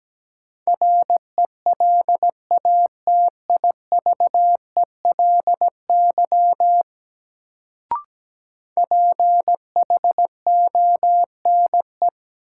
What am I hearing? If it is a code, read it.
Morse code, heard as a keyed tone, about 17 wpm, 700 Hz. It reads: RELATIVELY PHONE
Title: Sentences from Top 1100 Words - New Word Drill - 17wpm